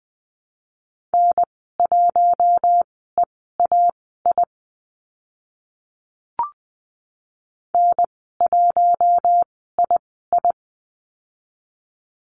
Morse code, heard as a keyed tone, about 20 wpm, spaced as for 14 wpm, 700 Hz. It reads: N1EAI N1II
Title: 042 Lesson 8g - Number 1 - Call Signs 20wpm